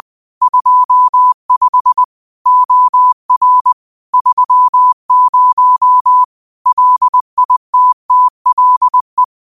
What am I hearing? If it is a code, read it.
Morse code, heard as a keyed tone, about 20 wpm, 1000 Hz. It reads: 25 OR 30 LITTLE
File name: qotdfiles / 20wpm